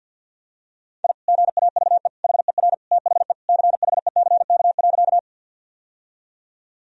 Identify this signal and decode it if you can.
Morse code, heard as a keyed tone, about 50 wpm, 700 Hz. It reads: I GAVE HER THE CHECK.
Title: Sentences from Top 1000 Words - Review Part 3 - 50wpm